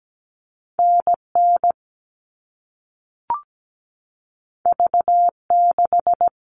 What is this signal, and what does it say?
Morse code, heard as a keyed tone, about 17 wpm, 700 Hz. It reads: NN V6
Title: All Cut Numbers - 2-Digit 17wpm